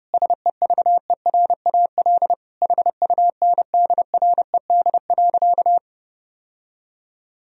Morse code, heard as a keyed tone, about 30 wpm, 700 Hz. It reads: SEVERAL HUNDRED.
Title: Sentences from Top 1000 Words - Review Part 2 - 30wpm